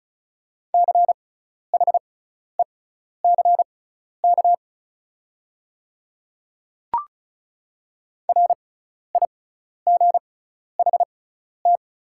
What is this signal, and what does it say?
Morse code, heard as a keyed tone, about 36 wpm, spaced as for 12 wpm, 690 Hz. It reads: CHECK RIGHT
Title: 6x Character Spacing - Top 500 Words - 35wpm